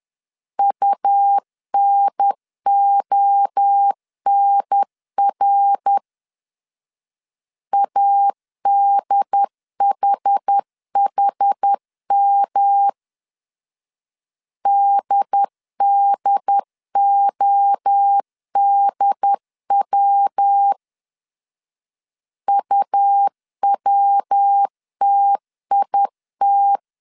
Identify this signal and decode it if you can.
Morse code, heard as a keyed tone, about 10 wpm, 800 Hz. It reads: UNONR ADHHM DDODW UWTIT
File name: CW / DET / 1 3 1500C S C 10wpm 3335